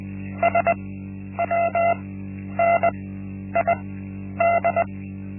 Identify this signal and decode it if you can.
Morse code, heard as a keyed tone, about 20 wpm, spaced as for 10 wpm, 680 Hz. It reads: SWNID
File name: Linksmakalnis-1